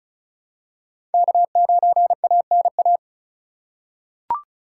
Morse code, heard as a keyed tone, about 34 wpm, 700 Hz. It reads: K9ANA
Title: USA Call Signs - 5 Characters - 2x Character Spacing - 35wpm